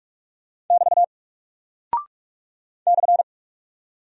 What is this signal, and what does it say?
Morse code, heard as a keyed tone, about 44 wpm, 700 Hz. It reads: <BT> /
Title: Single Character (Letter:Number:Punctuation:Prosign) Alt 45wpm